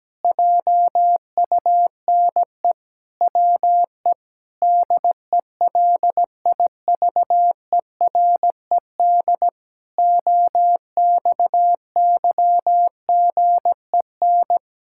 Morse code, heard as a keyed tone, about 17 wpm, 700 Hz. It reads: JUNE WE DELIVERED OXYGEN
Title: Pangram Challenge v2 - 17wpm